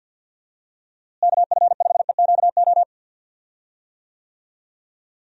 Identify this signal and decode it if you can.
Morse code, heard as a keyed tone, about 51 wpm, 700 Hz. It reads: KF5EQO
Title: USA Call Signs - 6 Characters - Repeated 2x - 50wpm